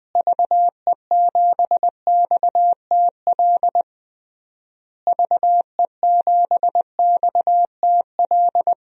Morse code, heard as a keyed tone, about 20 wpm, 700 Hz. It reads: VE7XTL VE7XTL
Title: DXCC-100 Call Signs - 6 Characters - Repeated 3x - 20wpm